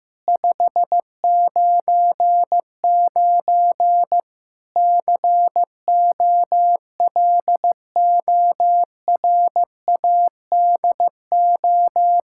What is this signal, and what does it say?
Morse code, heard as a keyed tone, about 15 wpm, 700 Hz. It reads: 599 COLORADO
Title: K1USN SST QSO Practice 15wpm